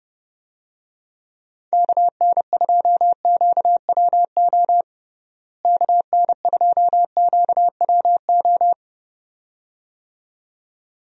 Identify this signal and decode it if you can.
Morse code, heard as a keyed tone, about 32 wpm, 700 Hz. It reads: KN2QWO KN2QWO
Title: USA Call Signs - 6 Characters - Repeated 2x - 30wpm